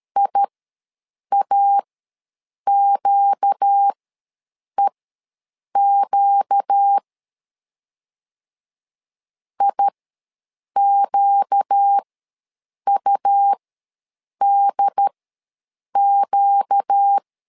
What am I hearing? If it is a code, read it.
Morse code, heard as a keyed tone, about 13 wpm, spaced as for 7 wpm, 800 Hz. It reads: IAQEQ IQUDQ
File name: LEZIONI / 4 Q ws 12wpm 1912